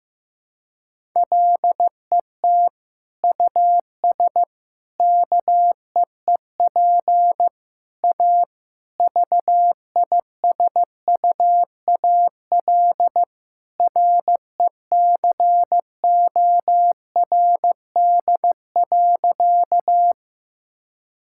Morse code, heard as a keyed tone, about 15 wpm, 700 Hz. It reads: LET US KEEP A VISUAL RECORD.
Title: Sentences from Top 1000 Words - Review Part 6 - 15wpm